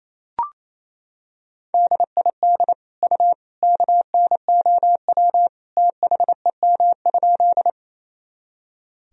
Morse code, heard as a keyed tone, about 28 wpm, 700 Hz. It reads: DID U KNOW THEM?
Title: Sentences from Top 100 Words and Abbreviations 28wpm